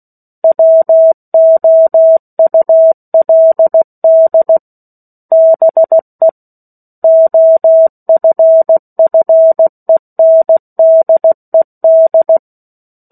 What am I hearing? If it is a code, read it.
Morse code, heard as a keyed tone, about 16 wpm, 650 Hz. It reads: WOULD BE OFFENDED
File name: 03-Chapter-03 16 WPM